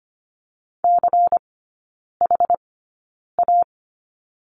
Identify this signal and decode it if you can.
Morse code, heard as a keyed tone, about 25 wpm, spaced as for 9 wpm, 700 Hz. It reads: CHA
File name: course / 25:10wpm / 093 Lesson 16e - Letter c - Three Characters 25wpm